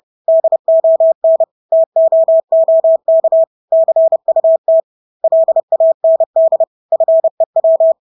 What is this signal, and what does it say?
Morse code, heard as a keyed tone, about 30 wpm, 650 Hz. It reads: DON TOOK CUT LAND FEW
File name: Engelske ord 30-30 wpm